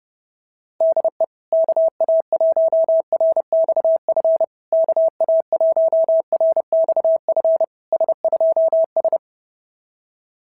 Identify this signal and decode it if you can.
Morse code, heard as a keyed tone, about 30 wpm, 650 Hz. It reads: DE KA1RXF KA1RXF S2S
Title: SOTA QSO Practice 30wpm